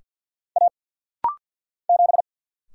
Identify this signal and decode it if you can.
Morse code, heard as a keyed tone, about 49 wpm, 700 Hz. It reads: A 7
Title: Single Letter-Number - Rapid-Fire 50wpm